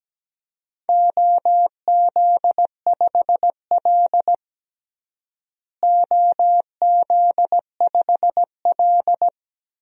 Morse code, heard as a keyed tone, about 17 wpm, 700 Hz. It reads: OZ5L OZ5L
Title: DXCC-100 Call Signs - 4 Characters - Repeated 2x - 17wpm